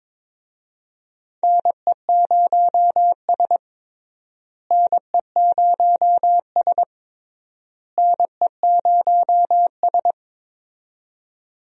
Morse code, heard as a keyed tone, about 23 wpm, 700 Hz. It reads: NE0S NE0S NE0S
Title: USA Call Signs - 4 Characters - Repeated 3x - 22wpm